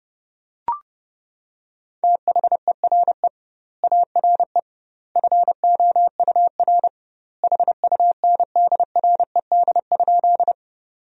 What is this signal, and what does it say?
Morse code, heard as a keyed tone, about 30 wpm, 700 Hz. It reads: THERE ARE FOUR HUNDRED?
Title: 2x Word-Spacing - Sentences from Top 400 Words - 30wpm